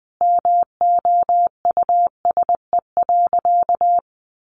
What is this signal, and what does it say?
Morse code, heard as a keyed tone, about 20 wpm, 700 Hz. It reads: MOUSE.
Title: Sentences from Top 700 Words 20wpm